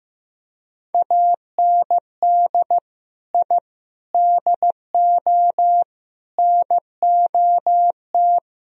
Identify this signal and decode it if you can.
Morse code, heard as a keyed tone, about 16 wpm, 700 Hz. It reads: AND I DO NOT
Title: Speed-Racing - Sets of 5 Words 15wpm